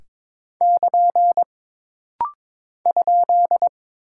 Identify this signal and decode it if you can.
Morse code, heard as a keyed tone, about 22 wpm, 700 Hz. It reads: <KN> ?
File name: practice / 22wpm / Single Punctuation-Prosign - Rapid-Fire - 22wpm